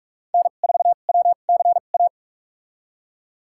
Morse code, heard as a keyed tone, about 46 wpm, spaced as for 32 wpm, 700 Hz. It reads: N4WCA
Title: USA Call Signs - 5 Characters - 2x Character Spacing - 45wpm